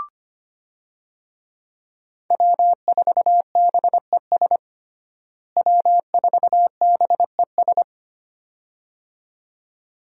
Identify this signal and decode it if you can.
Morse code, heard as a keyed tone, about 25 wpm, 700 Hz. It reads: W4BES W4BES
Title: USA Call Signs - 5 Characters - Repeated 2x - 25wpm